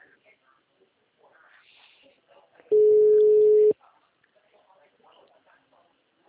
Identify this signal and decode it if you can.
Ringback tone (Europe, 425 Hz), with speech.